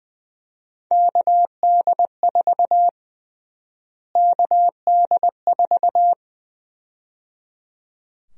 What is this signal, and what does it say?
Morse code, heard as a keyed tone, about 20 wpm, 700 Hz. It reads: KD4 KD4
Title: USA Call Sign Prefixes - Repeated 2x - 20wpm